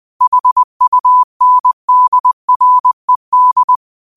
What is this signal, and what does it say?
Morse code, heard as a keyed tone, about 20 wpm, 1000 Hz. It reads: HUNDRED
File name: qotdfiles / 20wpm